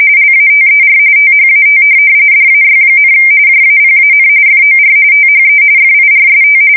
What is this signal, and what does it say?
RTTY, about 45.45 baud. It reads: LATER THAT DAY, JAPAN FOR THE FIRST
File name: fsk170 / USS-New-Jersey